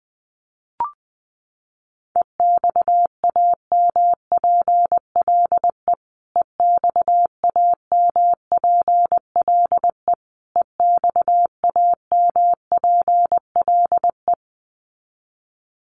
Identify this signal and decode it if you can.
Morse code, heard as a keyed tone, about 20 wpm, 700 Hz. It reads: EXAMPLE EXAMPLE EXAMPLE
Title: Top 500 Words Repeated 20wpm